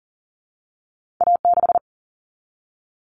Morse code, heard as a keyed tone, about 41 wpm, 700 Hz. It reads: A6
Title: All Cut Numbers - 2-Digit 40wpm